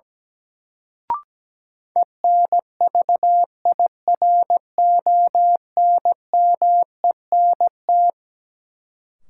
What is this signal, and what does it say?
Morse code, heard as a keyed tone, about 17 wpm, 700 Hz. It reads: ENVIRONMENT
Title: Sentences from Top 1000 Words - Review Part 5 - New Word Drill - 17wpm